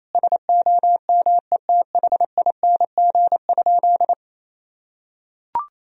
Morse code, heard as a keyed tone, about 28 wpm, 700 Hz. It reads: SOMETHING?
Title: Sentences from Top 1000 Words - Review Part 2 - 28wpm